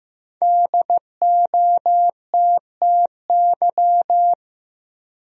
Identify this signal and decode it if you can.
Morse code, heard as a keyed tone, about 16 wpm, 700 Hz. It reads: DOTTY
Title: USA Common Names - Max 5 Letters - 15wpm